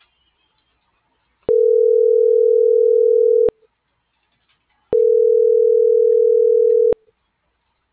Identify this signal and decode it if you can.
Ringback tone (North America), 440 + 480 Hz.